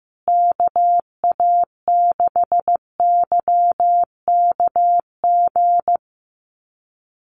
Morse code, heard as a keyed tone, about 15 wpm, 700 Hz. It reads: KA6YKG
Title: USA Call Signs Repeated 2x 15wpm